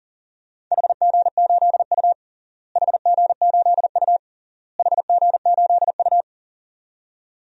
Morse code, heard as a keyed tone, about 40 wpm, 700 Hz. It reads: HG8U HG8U HG8U
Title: DXCC-100 Call Signs - 4 Characters - Repeated 3x - 40wpm